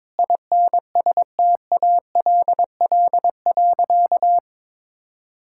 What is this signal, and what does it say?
Morse code, heard as a keyed tone, about 22 wpm, 700 Hz. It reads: INSTALL.
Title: Sentences from Top 1600 Words 22wpm